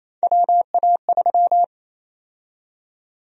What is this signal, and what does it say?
Morse code, heard as a keyed tone, about 28 wpm, 700 Hz. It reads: WA3
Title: USA Call Sign Prefixes - 28wpm